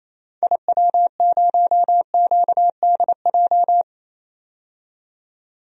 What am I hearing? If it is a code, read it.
Morse code, heard as a keyed tone, about 28 wpm, 700 Hz. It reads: IW0QDJ
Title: DXCC-100 Call Signs - 6 Characters - 28wpm